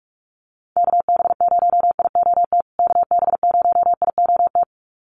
Morse code, heard as a keyed tone, about 44 wpm, 700 Hz. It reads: KB0IOT KB0IOT
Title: USA Call Signs Repeated 2x 45wpm